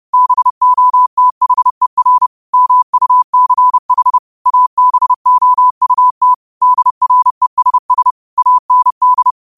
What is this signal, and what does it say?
Morse code, heard as a keyed tone, about 30 wpm, 1000 Hz. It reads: BOTHER MUCH ABOUT DRESS AND
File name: qotdfiles / 30wpm